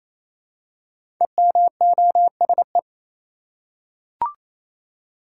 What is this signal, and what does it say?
Morse code, heard as a keyed tone, about 28 wpm, 700 Hz. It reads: EMOSE
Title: Code Groups - 5 Letters - 28wpm